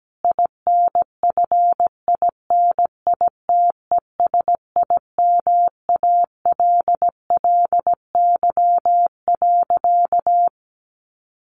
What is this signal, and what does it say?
Morse code, heard as a keyed tone, about 17 wpm, 700 Hz. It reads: INFINITESIMALLY.
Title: A Sound Of Thunder 17wpm